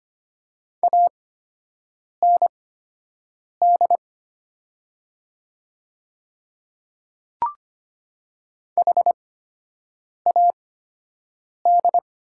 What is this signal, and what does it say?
Morse code, heard as a keyed tone, about 25 wpm, spaced as for 7 wpm, 700 Hz. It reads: AND HAD
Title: 8x Character Spacing - Top 100 Words - 25wpm